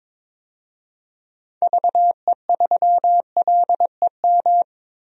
Morse code, heard as a keyed tone, about 21 wpm, 700 Hz. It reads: VE3LEM
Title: DXCC-100 Call Signs - 6 Characters - 22wpm